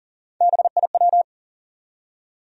Morse code, heard as a keyed tone, about 40 wpm, 700 Hz. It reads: BIW
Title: USA Call Sign Postfixes - 3 Letters 40wpm